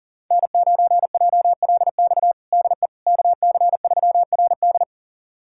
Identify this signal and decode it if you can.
Morse code, heard as a keyed tone, about 40 wpm, 700 Hz. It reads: N9JLX DE KC3RD
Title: USA Callsign DE Callsign 40wpm